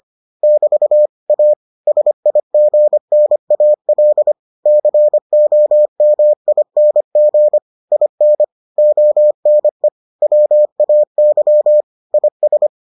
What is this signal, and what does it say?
Morse code, heard as a keyed tone, about 25 wpm, 600 Hz. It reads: <BT> A SIGNAL COMING IN ONE WAY IS